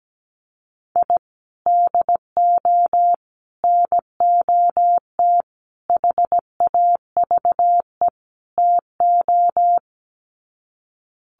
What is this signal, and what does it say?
Morse code, heard as a keyed tone, about 17 wpm, 700 Hz. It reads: I DO NOT HAVE TO
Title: Sets of 5 Words 17wpm